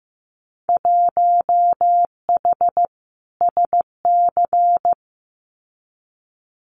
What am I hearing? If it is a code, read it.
Morse code, heard as a keyed tone, about 15 wpm, 700 Hz. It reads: 1H SC
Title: Winter Field Day Exchanges 15wpm